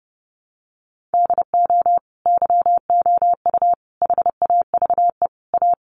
Morse code, heard as a keyed tone, about 28 wpm, 700 Hz. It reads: DO YOU HAVE A
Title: Sentences from Top 1400 Words 30wpm